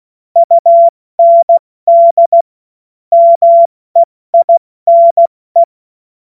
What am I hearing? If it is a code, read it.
Morse code, heard as a keyed tone, about 16 wpm, 680 Hz. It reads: UND MEINE
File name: Morsekurs / Witze / Witz 07. September 2021, 70 F80